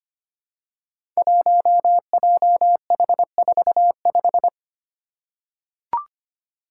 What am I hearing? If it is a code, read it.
Morse code, heard as a keyed tone, about 25 wpm, 700 Hz. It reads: 1JH45
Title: Five Challenging Letter-Number Combinations 25wpm